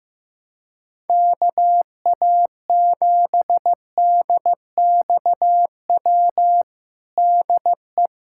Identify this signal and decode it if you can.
Morse code, heard as a keyed tone, about 15 wpm, 700 Hz. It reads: KA7DXW DE
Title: USA Callsign DE Callsign 15wpm